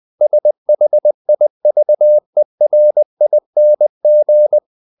Morse code, heard as a keyed tone, about 20 wpm, 600 Hz. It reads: SHIVERING
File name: the old man and the sea / the old man and the sea part 1 20wpm